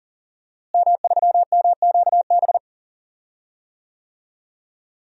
Morse code, heard as a keyed tone, about 40 wpm, 700 Hz. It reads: M3MQB
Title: DXCC-100 Call Signs - 5 Characters - Repeated 3x - 40wpm